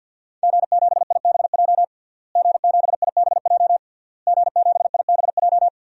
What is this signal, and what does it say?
Morse code, heard as a keyed tone, about 50 wpm, 700 Hz. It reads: G7IBJ G7IBJ G7IBJ
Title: DXCC-100 Call Signs - 5 Characters - Repeated 3x - 50wpm